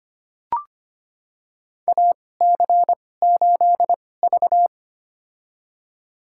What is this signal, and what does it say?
Morse code, heard as a keyed tone, about 25 wpm, spaced as for 18 wpm, 700 Hz. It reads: AC8V
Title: USA Call Signs - 4 Characters - 2x Character Spacing - 25wpm